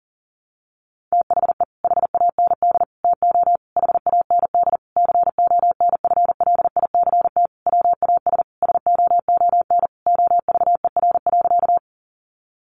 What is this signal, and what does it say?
Morse code, heard as a keyed tone, about 37 wpm, 700 Hz. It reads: THE HAND TO HAND CONFLICT WAS SOON OVER.
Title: Sentences from Top 1300 Words 40wpm